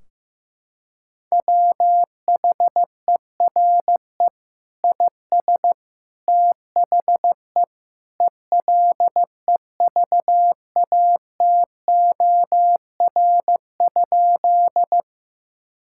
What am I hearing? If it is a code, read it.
Morse code, heard as a keyed tone, about 16 wpm, 700 Hz. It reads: WHERE IS THE ELEVATOR?
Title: Sentences from Top 1600 Words 15wpm